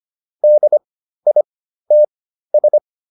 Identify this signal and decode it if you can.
Morse code, heard as a keyed tone, about 25 wpm, spaced as for 13 wpm, 600 Hz. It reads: DITS